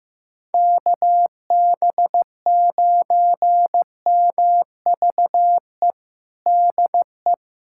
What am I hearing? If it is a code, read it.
Morse code, heard as a keyed tone, about 15 wpm, 700 Hz. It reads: KB9MVE DE
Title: USA Callsign DE Callsign 15wpm